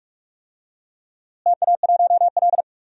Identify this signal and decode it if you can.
Morse code, heard as a keyed tone, about 46 wpm, 700 Hz. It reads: TA1L
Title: DXCC-100 Call Signs - 4 Characters - Repeated 2x - 45wpm